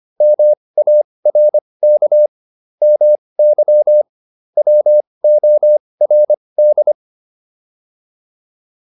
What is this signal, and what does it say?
Morse code, heard as a keyed tone, about 25 wpm, spaced as for 20 wpm, 600 Hz. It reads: MARK MY WORD
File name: PR304 20